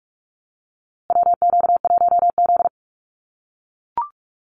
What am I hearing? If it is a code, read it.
Morse code, heard as a keyed tone, about 44 wpm, 700 Hz. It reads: WQ1Z
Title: USA Call Signs Repeated 2x 45wpm